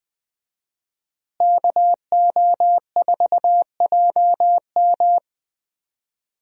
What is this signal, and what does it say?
Morse code, heard as a keyed tone, about 20 wpm, 700 Hz. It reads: KO4JM
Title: USA Call Signs - 5 Characters - Repeated 2x - 20wpm